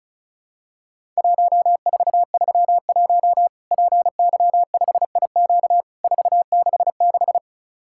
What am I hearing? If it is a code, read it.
Morse code, heard as a keyed tone, about 36 wpm, 700 Hz. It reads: 1431 PY5IQ 466
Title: Worked All Europe DX Contest - QTC Exchange - 35wpm